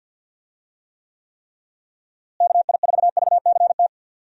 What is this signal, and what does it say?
Morse code, heard as a keyed tone, about 52 wpm, 700 Hz. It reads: KI4VCT
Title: USA Call Signs - 6 Characters - Repeated 3x - 50wpm